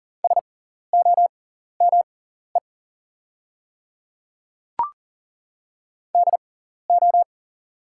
Morse code, heard as a keyed tone, about 40 wpm, spaced as for 14 wpm, 690 Hz. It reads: SOME DO
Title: 6x Character Spacing - Top 200 Words - 40wpm